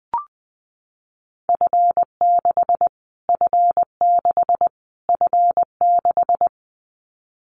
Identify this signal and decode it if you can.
Morse code, heard as a keyed tone, about 20 wpm, 700 Hz. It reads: F6 F6 F6
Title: Callsign Prefixes for Top 25 DXCC - Repeated 3x - 20wpm